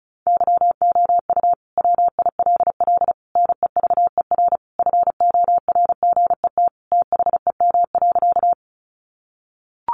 Morse code, heard as a keyed tone, about 34 wpm, 700 Hz. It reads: YOU WILL NEVER FORGET THEM.
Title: Sentences from Top 1200 Words 35wpm